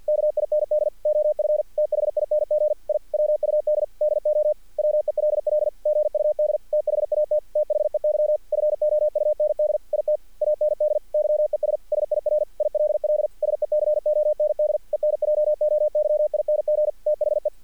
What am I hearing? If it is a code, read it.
Morse code, heard as a keyed tone, about 46 wpm, 600 Hz. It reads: KIND OF THING I WUD DO WELL GUD THAT THEY FOUND IT AND YES SIR ILL SECOND ENJOYING THE
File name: archived / CRSnet